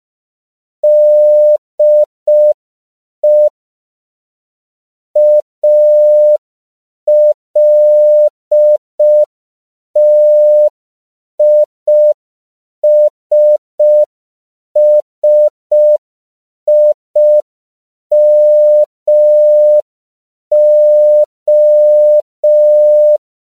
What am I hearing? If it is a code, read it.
Morse code, heard as a keyed tone, about 5 wpm, 600 Hz. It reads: DE ALTISSIMO
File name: CW / 5PPM